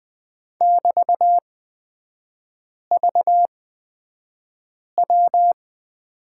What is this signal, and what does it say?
Morse code, heard as a keyed tone, about 20 wpm, spaced as for 5 wpm, 700 Hz. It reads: <BT>VW
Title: 224 Lesson 36e - BT - Three Characters 20wpm